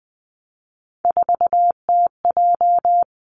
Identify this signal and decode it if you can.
Morse code, heard as a keyed tone, about 20 wpm, 700 Hz. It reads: 4TJ
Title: Three Challenging Letter-Number Combinations 20wpm